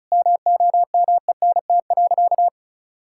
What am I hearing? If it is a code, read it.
Morse code, heard as a keyed tone, about 36 wpm, 700 Hz. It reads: MOMENT.
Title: Sentences from Top 1000 Words - Review Part 5 - 35wpm